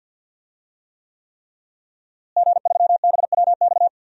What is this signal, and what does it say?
Morse code, heard as a keyed tone, about 51 wpm, 700 Hz. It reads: G3BWX
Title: DXCC-100 Call Signs - 5 Characters - Repeated 3x - 50wpm